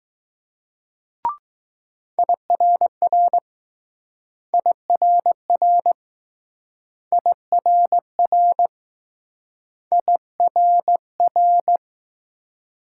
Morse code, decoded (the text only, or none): IRR IRR IRR IRR